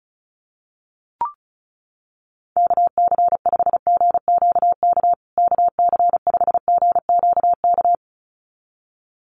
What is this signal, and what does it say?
Morse code, heard as a keyed tone, about 34 wpm, 700 Hz. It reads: KC5GQK KC5GQK
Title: USA Call Signs Repeated 2x 35wpm